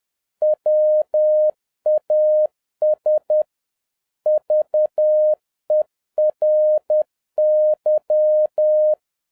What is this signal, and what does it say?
Morse code, heard as a keyed tone, about 10 wpm, 610 Hz. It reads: WAS VERY